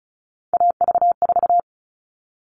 Morse code, heard as a keyed tone, about 34 wpm, 700 Hz. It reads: AV4
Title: All Cut Numbers - 3-Digit 35wpm